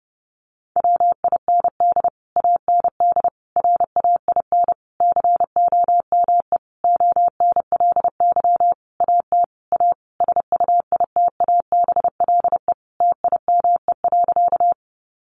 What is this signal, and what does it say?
Morse code, heard as a keyed tone, about 30 wpm, 700 Hz. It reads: WIND AND RAIN COME ONLY AT A SUITABLE TIME.
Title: Sentences from Top 1400 Words 30wpm